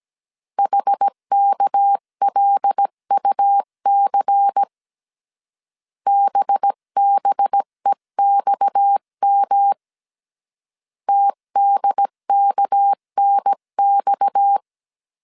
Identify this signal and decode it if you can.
Morse code, heard as a keyed tone, about 17 wpm, 800 Hz. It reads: HKLUC BBEXM TDKNX
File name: CW / DET / 1 6 1500C S C 16wpm 2733